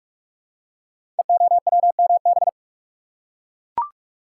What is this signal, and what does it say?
Morse code, heard as a keyed tone, about 44 wpm, 700 Hz. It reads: EOWMB
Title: Code Groups - 5 Letter:Numbers - 45wpm